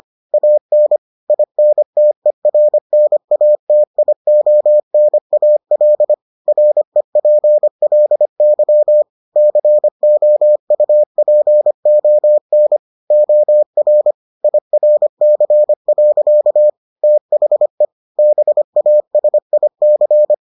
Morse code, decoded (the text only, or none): AN INTERNATIONAL REPLY COUPON OR IRC. THE BASIC